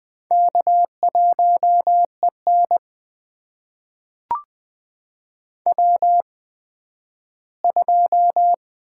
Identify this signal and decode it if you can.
Morse code, heard as a keyed tone, about 20 wpm, spaced as for 12 wpm, 700 Hz. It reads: K1EN W2
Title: USA Call Signs - 4 Characters - 8x Character Spacing - 20wpm